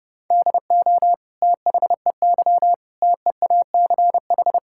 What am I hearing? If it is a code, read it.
Morse code, heard as a keyed tone, about 30 wpm, 700 Hz. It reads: DO THEY TEACH
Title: Sentences from Top 1000 Words - Review Part 6 - 30wpm